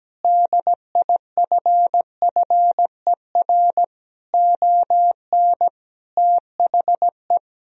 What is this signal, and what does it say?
Morse code, heard as a keyed tone, about 17 wpm, 700 Hz. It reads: DIFFER ON THE
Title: Sentences from Top 1000 Words - Review Part 1 - 17wpm